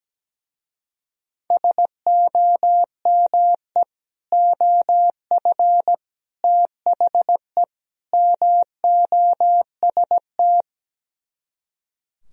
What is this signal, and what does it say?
Morse code, heard as a keyed tone, about 17 wpm, 700 Hz. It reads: SOME OF THE MOST
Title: Speed-Racing - Sets of 4 Words 17wpm